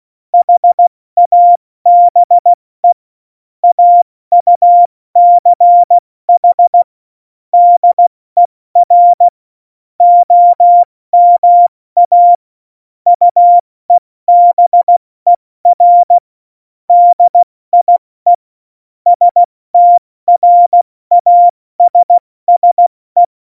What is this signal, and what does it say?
Morse code, heard as a keyed tone, about 16 wpm, spaced as for 14 wpm, 700 Hz. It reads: HABE AUCH DER OMA UEBER DIE STRASSE
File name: Morsekurs / Witze / Witz 30. April 2019, 70 F80